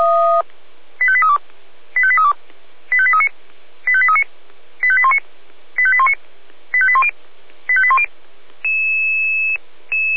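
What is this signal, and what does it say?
Five-tone selective call (ZVEI), heard as quick runs of five tones, tones of about 70 ms. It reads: 85832 85832 85839 85839 85819 85819 85810 85810
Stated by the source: ZVEI five-tone alert received by SDR, feuerwehr-orlamuende.org fire brigade audio server.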